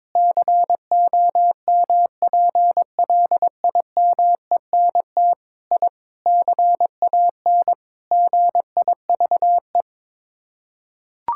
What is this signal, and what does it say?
Morse code, heard as a keyed tone, about 22 wpm, 700 Hz. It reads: COMPLIMENT I CAN GIVE
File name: practice / 22wpm / Sets of 8 Words - 22wpm